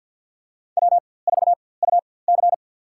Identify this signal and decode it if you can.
Morse code, heard as a keyed tone, about 51 wpm, spaced as for 23 wpm, 700 Hz. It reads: W4UC
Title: USA Call Signs - 4x Character Spacing - 50wpm